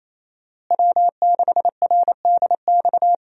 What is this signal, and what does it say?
Morse code, heard as a keyed tone, about 28 wpm, 700 Hz. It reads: W6RDX
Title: USA Call Signs - 5 Characters - Repeated 2x - 28wpm